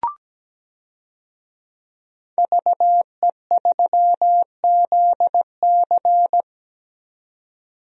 Morse code, heard as a keyed tone, about 17 wpm, 700 Hz. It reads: VE3ZC
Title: DXCC-100 Call Signs - 5 Characters - 17wpm